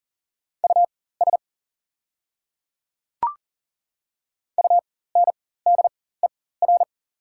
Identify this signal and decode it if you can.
Morse code, heard as a keyed tone, about 40 wpm, spaced as for 19 wpm, 700 Hz. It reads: US UNDER
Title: 4x Character Spacing - Top 200 Words - 40wpm